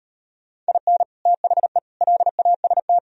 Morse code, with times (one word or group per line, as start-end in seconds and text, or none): 0.7-1.0 IN
1.3-1.8 THE
2.0-3.0 LAST